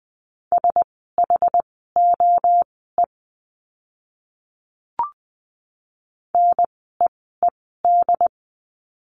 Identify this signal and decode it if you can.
Morse code, heard as a keyed tone, about 20 wpm, spaced as for 14 wpm, 700 Hz. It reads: SHOE NEED
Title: 068 Lesson 12f - Letter d - Words 20wpm